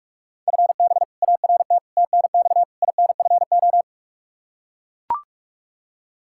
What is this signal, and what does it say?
Morse code, heard as a keyed tone, about 41 wpm, 700 Hz. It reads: FB ART TNX INFO
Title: K1USN SST QSO Practice 45wpm